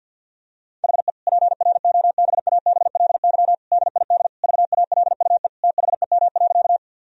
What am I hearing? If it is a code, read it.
Morse code, heard as a keyed tone, about 50 wpm, 700 Hz. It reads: HE PROBABLY DID VALUE THEM.
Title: Sentences from Top 1000 Words - Review Part 4 - 50wpm